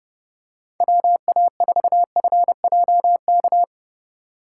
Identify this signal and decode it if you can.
Morse code, heard as a keyed tone, about 30 wpm, 700 Hz. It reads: WA4FJK